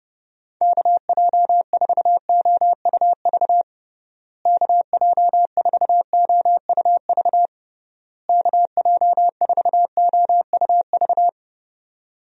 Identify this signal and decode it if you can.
Morse code, heard as a keyed tone, about 30 wpm, 700 Hz. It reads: KJ4OUV KJ4OUV KJ4OUV
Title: USA Call Signs - 6 Characters - Repeated 3x - 30wpm